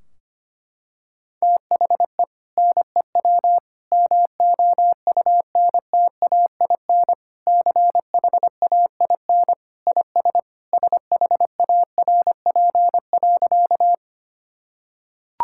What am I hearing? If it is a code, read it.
Morse code, heard as a keyed tone, about 25 wpm, 700 Hz. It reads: THE NEW MOUNTAIN CHAIN IS SHARP.
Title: Sentences from Top 1600 Words 25wpm